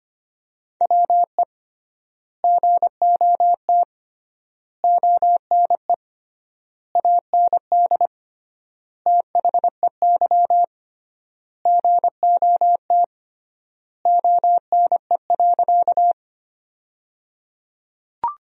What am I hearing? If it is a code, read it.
Morse code, heard as a keyed tone, about 25 wpm, 700 Hz. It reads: WE GOT ONE AND THEY GOT ONE.
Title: 3x Word-Spacing - Sentences from Top 300 Words - 25wpm